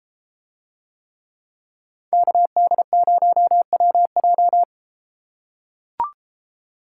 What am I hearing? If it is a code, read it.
Morse code, heard as a keyed tone, about 31 wpm, 700 Hz. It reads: KD0WJ